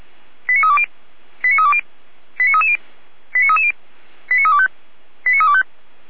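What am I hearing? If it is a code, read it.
Five-tone selective call (ZVEI), heard as quick runs of five tones, tones of about 70 ms. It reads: 89320 89320 89330 89330 89325 89325
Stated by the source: SDR receiver recording of ZVEI five-tone selective call 89315, from feuerwehr-orlamuende.org (fire brigade Orlamünde).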